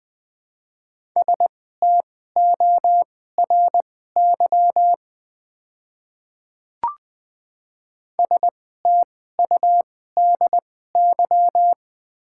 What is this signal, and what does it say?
Morse code, heard as a keyed tone, about 20 wpm, spaced as for 15 wpm, 700 Hz. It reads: STORY STUDY
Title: 2x Character Spacing - Top 300 Words - 20wpm